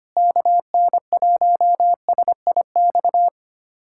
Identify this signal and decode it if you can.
Morse code, heard as a keyed tone, about 25 wpm, 700 Hz. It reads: KN1SIX
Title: USA Callsign DE Callsign - 6 Characters - 25wpm